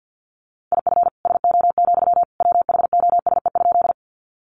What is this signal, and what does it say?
Morse code, heard as a keyed tone, about 50 wpm, 700 Hz. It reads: IF SO, WHOSE?
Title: Sentences from Top 1200 Words 50wpm